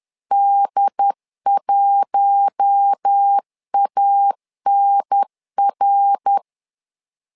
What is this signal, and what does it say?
Morse code, heard as a keyed tone, about 10 wpm, 800 Hz. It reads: D1ANR
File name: LEZIONI / 4 Q1 10wpm 2448